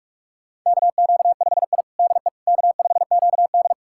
Morse code, heard as a keyed tone, about 45 wpm, 700 Hz. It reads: KQ5I DE K5QD
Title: USA Callsign DE Callsign - 4 Characters - 45wpm